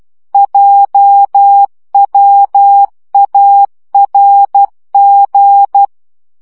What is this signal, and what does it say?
Morse code, heard as a keyed tone, about 12 wpm, 800 Hz. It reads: JWARG
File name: morze vizsga 2011-2 10